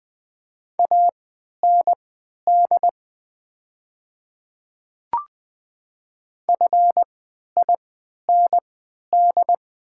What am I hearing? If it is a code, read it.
Morse code, heard as a keyed tone, about 20 wpm, spaced as for 11 wpm, 700 Hz. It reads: AND FIND
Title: 3x Character Spacing - Top 200 Words - 20wpm